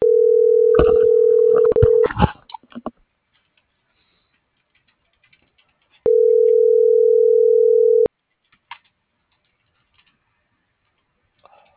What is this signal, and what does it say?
Ringback tone (North America), 440 + 480 Hz.